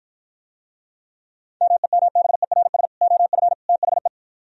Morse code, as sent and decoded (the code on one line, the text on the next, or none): -- . -- -... .-. ... / --- ..-. / - .... .
MEMBERS OF THE